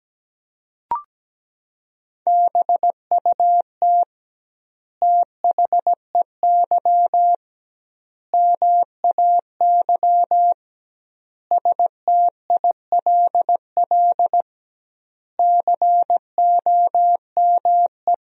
Morse code, heard as a keyed tone, about 17 wpm, 700 Hz. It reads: BUT THEY MAY STILL COME
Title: 2x Word-Spacing - Sentences from Top 300 Words - 17wpm